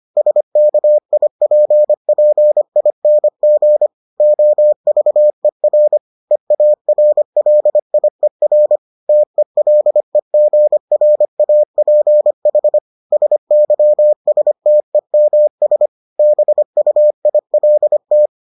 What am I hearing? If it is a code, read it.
Morse code, heard as a keyed tone, about 25 wpm, 600 Hz. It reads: SKIPPING OVER EARLIER TELEGRAPH SYSTEMS BUILT